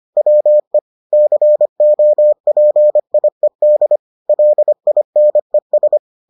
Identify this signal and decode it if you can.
Morse code, heard as a keyed tone, about 25 wpm, 600 Hz. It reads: WE COPIED LINES